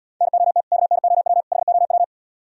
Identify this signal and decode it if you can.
Morse code, heard as a keyed tone, about 77 wpm, 700 Hz. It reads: NOT KNOW HOW